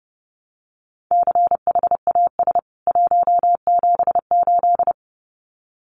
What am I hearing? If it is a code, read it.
Morse code, heard as a keyed tone, about 30 wpm, 700 Hz. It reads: CHAS 178
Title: CWT Exchange 30wpm